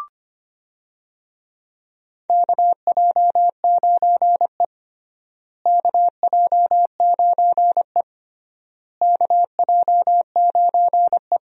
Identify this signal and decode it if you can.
Morse code, heard as a keyed tone, about 25 wpm, 700 Hz. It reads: KJ9E KJ9E KJ9E
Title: USA Call Signs - 4 Characters - Repeated 3x - 25wpm